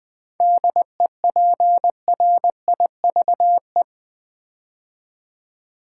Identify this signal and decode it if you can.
Morse code, heard as a keyed tone, about 20 wpm, 700 Hz. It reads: DEPRIVE
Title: 7 Letter Words - First 3 Letters Given - 20wpm